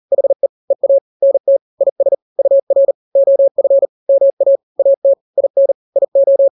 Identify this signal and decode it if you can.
Morse code, heard as a keyed tone, about 40 wpm, 550 Hz. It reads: HE EA NT IS UR OF MA AT IN IO
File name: n-grams / 2-grams 40 35 WPM